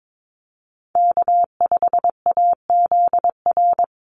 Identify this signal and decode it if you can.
Morse code, heard as a keyed tone, about 22 wpm, 700 Hz. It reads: K5AZR
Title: Call Signs Repeated 2x - 22wpm